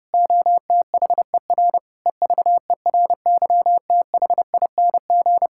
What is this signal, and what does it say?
Morse code, heard as a keyed tone, about 30 wpm, 700 Hz. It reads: OTHER EVERYTHING
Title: Sentences from Top 1600 Words 30wpm